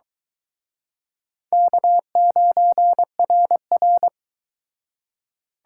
Morse code, heard as a keyed tone, about 26 wpm, 700 Hz. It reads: K9RR